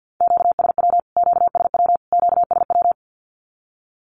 Morse code, heard as a keyed tone, about 50 wpm, 700 Hz. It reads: QSW QSW QSW
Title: Q-Codes Repeated 3x 50wpm